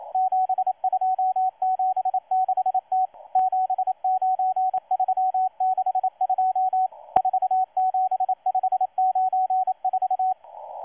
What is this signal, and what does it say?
Morse code, heard as a keyed tone, about 27 wpm, 750 Hz, with radio noise. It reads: 7276T 79362 47594
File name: M12-2016-07-07-1310utc-13926khz-error-restart